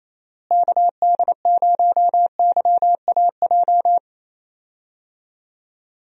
Morse code, heard as a keyed tone, about 28 wpm, 700 Hz. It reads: KD0YAJ
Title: USA Call Signs - 6 Characters - Repeated 3x - 28wpm